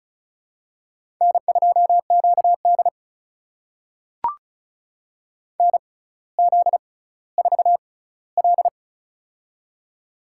Morse code, heard as a keyed tone, about 36 wpm, spaced as for 18 wpm, 700 Hz. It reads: N2QD NZ4L
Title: USA Call Signs - 4 Characters - 6x Character Spacing - 35wpm